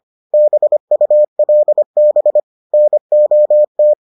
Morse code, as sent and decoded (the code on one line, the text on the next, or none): -... ..- .-.. -... / -. --- -
BULB NOT